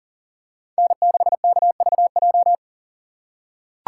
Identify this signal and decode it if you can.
Morse code, heard as a keyed tone, about 40 wpm, 700 Hz. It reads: N6KVJ
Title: USA Call Signs - 5 Characters - 4x Character Spacing - 40wpm